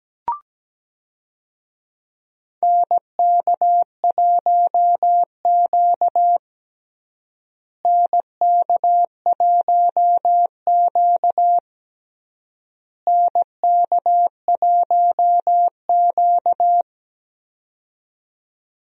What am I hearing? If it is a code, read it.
Morse code, heard as a keyed tone, about 17 wpm, 700 Hz. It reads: NK1Q NK1Q NK1Q
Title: USA Call Signs - 4 Characters - Repeated 3x - 17wpm